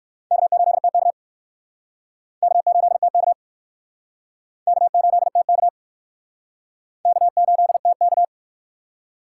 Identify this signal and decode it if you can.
Morse code, heard as a keyed tone, about 60 wpm, 700 Hz. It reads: K8TK K8TK K8TK K8TK